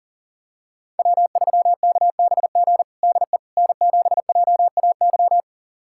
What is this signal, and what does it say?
Morse code, heard as a keyed tone, about 40 wpm, 700 Hz. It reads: W3KBG DE N7JAY
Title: USA Callsign DE Callsign - 5 Characters - 40wpm